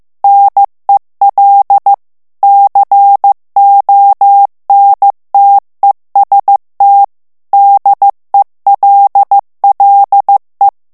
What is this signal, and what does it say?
Morse code, heard as a keyed tone, about 15 wpm, 800 Hz. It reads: NEL CONTEST DELLE
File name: morse mp3 / ITALIEN texte clair 2 vitesse 14 mots minute